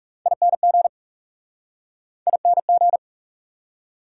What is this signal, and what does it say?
Morse code, heard as a keyed tone, about 38 wpm, 700 Hz. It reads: ING ING